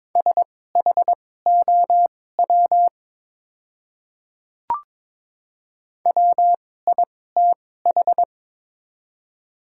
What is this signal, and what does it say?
Morse code, heard as a keyed tone, about 22 wpm, spaced as for 16 wpm, 700 Hz. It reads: SHOW WITH
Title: 2x Character Spacing - Top 200 Words - 22wpm